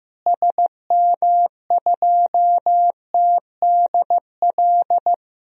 Morse code, heard as a keyed tone, about 15 wpm, 700 Hz. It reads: SM2TDL
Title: DXCC-100 Call Signs - 6 Characters - Repeated 3x - 15wpm